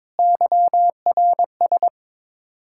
Morse code, heard as a keyed tone, about 22 wpm, 700 Hz. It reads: YRS
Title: Practice QSO - Exchange 3 - More Info - 22wpm